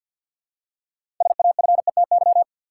Morse code, heard as a keyed tone, about 53 wpm, 700 Hz. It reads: SAFETY
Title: Six Letter Words - 50wpm